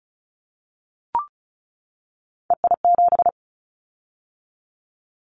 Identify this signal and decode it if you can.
Morse code, heard as a keyed tone, about 36 wpm, 700 Hz. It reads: EI7
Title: Callsign Prefixes for Top 50 DXCC - 35wpm